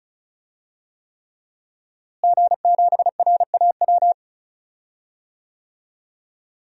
Morse code, heard as a keyed tone, about 36 wpm, 700 Hz. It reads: G7RAW
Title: DXCC-100 Call Signs - 5 Characters - Repeated 3x - 35wpm